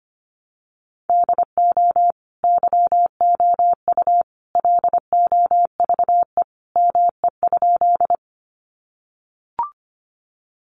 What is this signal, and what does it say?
Morse code, heard as a keyed tone, about 25 wpm, 700 Hz. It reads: DO YOU LOVE ME?
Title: Sentences from Top 400 Words 25wpm